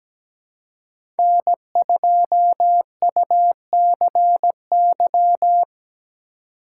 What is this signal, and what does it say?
Morse code, heard as a keyed tone, about 17 wpm, 700 Hz. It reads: N2UCY
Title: USA Call Signs - 2x Character Spacing - 17wpm